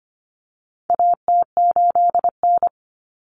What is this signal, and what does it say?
Morse code, heard as a keyed tone, about 25 wpm, 700 Hz. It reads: AT8N
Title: Common Cut Numbers - 4-Digit 25wpm